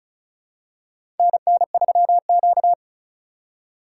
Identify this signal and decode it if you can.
Morse code, heard as a keyed tone, about 34 wpm, 700 Hz. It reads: NN3Q
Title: USA Call Signs - 4 Characters - 2x Character Spacing - 35wpm